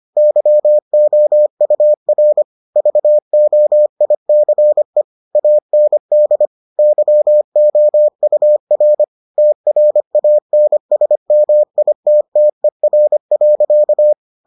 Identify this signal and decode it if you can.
Morse code, heard as a keyed tone, about 25 wpm, 600 Hz. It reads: YOUR VOICE AND YOUR TRANSMITTER.